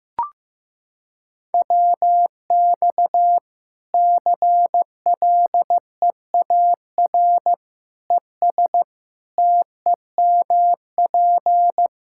Morse code, heard as a keyed tone, about 15 wpm, 700 Hz. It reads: WX CLEAR ES TEMP
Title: Practice QSO - Exchange 2 - Info - 15wpm